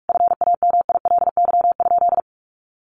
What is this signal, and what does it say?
Morse code, heard as a keyed tone, about 44 wpm, 700 Hz. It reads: FAMILY?
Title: Sentences from Top 1300 Words 45wpm